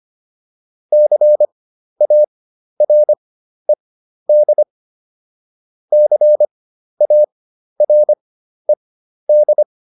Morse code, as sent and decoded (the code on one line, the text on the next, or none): -.-. .- .-. -.. / -.-. .- .-. -..
CARED CARED